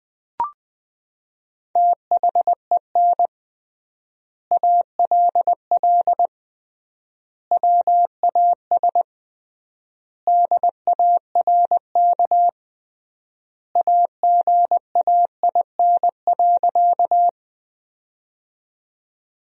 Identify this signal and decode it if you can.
Morse code, heard as a keyed tone, about 20 wpm, 700 Hz. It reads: THEN ALL WAS DARK AGAIN.
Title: 3x Word-Spacing - Sentences from Top 500 Words - 20wpm